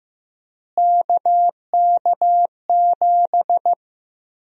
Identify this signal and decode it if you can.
Morse code, heard as a keyed tone, about 16 wpm, 700 Hz. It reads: KK7
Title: USA Call Sign Prefixes - Repeated 2x - 15wpm